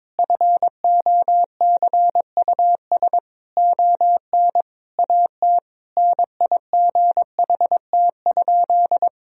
Morse code, heard as a keyed tone, about 21 wpm, 700 Hz. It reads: FOCUS ON AT NIGHT?
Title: Sentences from Top 1000 Words - Review Part 4 - 22wpm